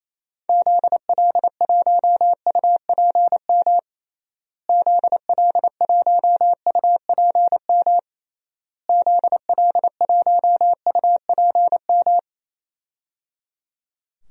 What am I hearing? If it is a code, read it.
Morse code, heard as a keyed tone, about 28 wpm, 700 Hz. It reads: ZL1UPM ZL1UPM ZL1UPM
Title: DXCC-100 Call Signs - 6 Characters - Repeated 3x - 28wpm